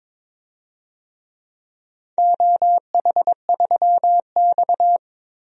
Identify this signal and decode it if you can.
Morse code, heard as a keyed tone, about 22 wpm, 700 Hz. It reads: OH3X
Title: DXCC-100 Call Signs - 4 Characters - 22wpm